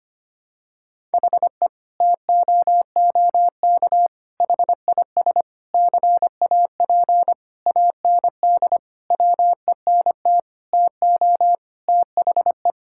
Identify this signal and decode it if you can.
Morse code, heard as a keyed tone, about 25 wpm, 700 Hz. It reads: HE TOOK HIS CAP AND WENT TO THE
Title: Sentences from Top 1600 Words 25wpm